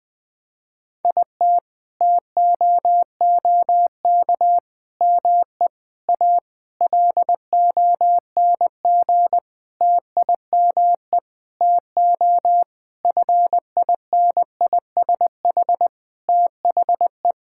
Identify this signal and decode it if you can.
Morse code, heard as a keyed tone, about 21 wpm, 700 Hz. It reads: IT TOOK ME A LONG TIME TO FINISH THE
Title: Sentences with 7-Word Phrases - 20wpm